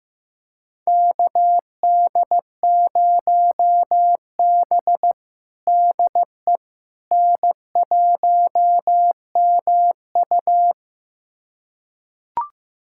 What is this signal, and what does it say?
Morse code, heard as a keyed tone, about 16 wpm, 700 Hz. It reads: KD0B DE N1MU
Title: USA Callsign DE Callsign - 4 Characters - 15wpm